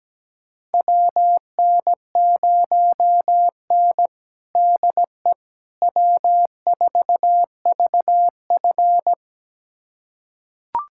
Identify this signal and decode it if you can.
Morse code, heard as a keyed tone, about 17 wpm, 700 Hz. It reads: WN0N DE W4VF
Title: USA Callsign DE Callsign - 4 Characters - 17wpm